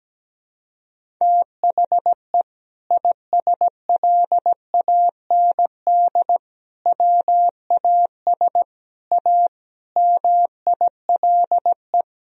Morse code, heard as a keyed tone, about 17 wpm, 700 Hz. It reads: THE ISLAND WAS A MILE
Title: Sentences from Top 500 Words and Abbreviations 17wpm